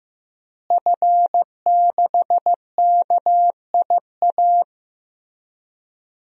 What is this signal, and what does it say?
Morse code, heard as a keyed tone, about 15 wpm, 700 Hz. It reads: F6KIA
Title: DXCC-100 Call Signs - 5 Characters - Repeated 3x - 15wpm